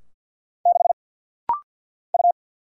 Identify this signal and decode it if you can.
Morse code, heard as a keyed tone, about 50 wpm, 700 Hz. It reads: 6 U
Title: Single Letter-Number - Rapid-Fire 50wpm